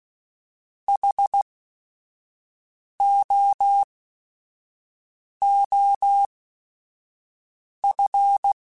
Morse code, heard as a keyed tone, about 16 wpm, spaced as for 5 wpm, 800 Hz. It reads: HOOF